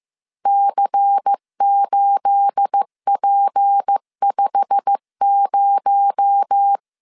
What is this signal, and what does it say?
Morse code, heard as a keyed tone, about 15 wpm, 800 Hz. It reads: C8P50
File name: CW / DET / 1 9 1500C S A 14wpm 3222